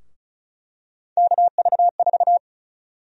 Morse code, heard as a keyed tone, about 34 wpm, 700 Hz. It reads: KV4
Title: USA Call Sign Prefixes - Repeated 2x - 35wpm